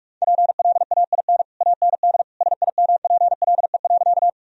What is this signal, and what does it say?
Morse code, heard as a keyed tone, about 41 wpm, 700 Hz. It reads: PLAIN AND SIMPLE.
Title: Sentences from Top 1000 Words - Review Part 2 - 45wpm